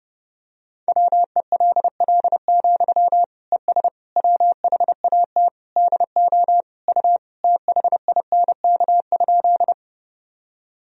Morse code, heard as a keyed tone, about 30 wpm, 700 Hz. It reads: WELL, ES WHAT DO U THINK?
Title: Sentences from Top 200 Words and Abbreviations 30wpm